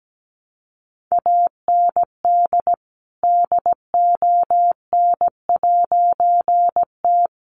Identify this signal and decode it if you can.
Morse code, heard as a keyed tone, about 17 wpm, 700 Hz. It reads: AND DON'T
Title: A Sound Of Thunder 17wpm